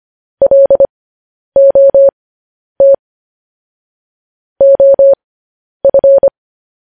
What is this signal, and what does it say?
Morse code, heard as a keyed tone, about 25 wpm, spaced as for 10 wpm, 550 Hz. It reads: LOT OF